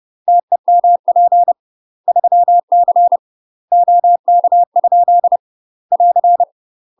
Morse code, heard as a keyed tone, about 30 wpm, 700 Hz. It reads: TEMP 3C OK? <AR>
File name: UK MASTER QSOs TRAINING / 30WPM / UK MASTER QSOs 0038